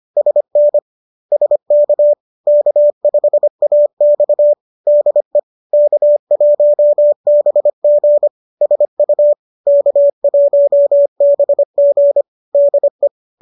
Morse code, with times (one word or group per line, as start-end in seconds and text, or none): 0.2-0.8 SN
1.3-2.1 SK
2.5-4.5 K5AX
4.9-5.4 DE
5.7-8.3 K1BG
8.6-9.3 SU
9.7-12.2 K1BG
12.5-13.1 DE